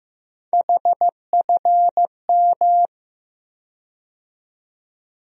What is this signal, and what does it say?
Morse code, heard as a keyed tone, about 15 wpm, 700 Hz. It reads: HFM
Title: USA Call Sign Postfixes - 3 Letters 15wpm